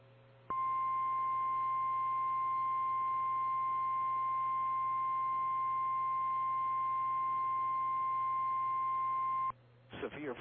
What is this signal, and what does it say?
Emergency Alert System (SAME) tones (the 1050 Hz attention tone). It is heard from 0.5 to 9.5 s.
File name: eas / KKPRKXPN / KKPRKXPN